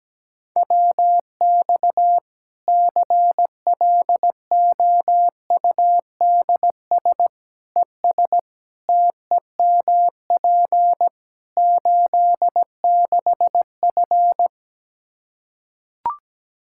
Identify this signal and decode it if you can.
Morse code, heard as a keyed tone, about 17 wpm, 700 Hz. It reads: WX CLOUDS ES TEMP 86F
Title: Practice QSO - Exchange 2 - Info - 17wpm